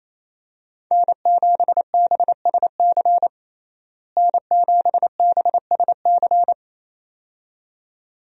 Morse code, heard as a keyed tone, about 28 wpm, 700 Hz. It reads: N7BSC N7BSC
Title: USA Call Signs - 5 Characters - Repeated 2x - 28wpm